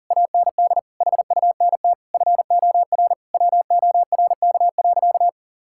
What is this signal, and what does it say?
Morse code, heard as a keyed tone, about 40 wpm, 700 Hz. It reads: AND HUNT FOR WORK.
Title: Sentences from Top 1600 Words 40wpm